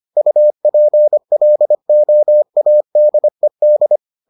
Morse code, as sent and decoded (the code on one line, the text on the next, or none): ..- .--. .-.. --- .- -.. -..
UPLOADED